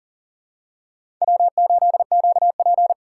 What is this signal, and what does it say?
Morse code, heard as a keyed tone, about 40 wpm, 700 Hz. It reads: W8QP
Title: USA Call Signs - 4 Characters - 2x Character Spacing - 40wpm